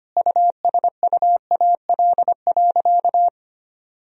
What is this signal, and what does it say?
Morse code, heard as a keyed tone, about 25 wpm, 700 Hz. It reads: USUAL.
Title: Sentences from Top 1000 Words - Review Part 2 - 25wpm